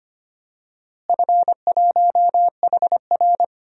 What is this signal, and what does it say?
Morse code, heard as a keyed tone, about 25 wpm, 700 Hz. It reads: F1HR
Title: DXCC-100 Call Signs - 4 Characters - Repeated 3x - 25wpm